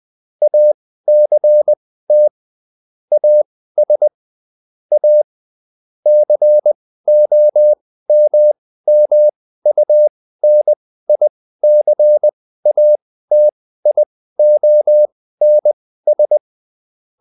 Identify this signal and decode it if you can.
Morse code, heard as a keyed tone, about 20 wpm, spaced as for 14 wpm, 600 Hz. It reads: ACT AS A COMMUNICATIONS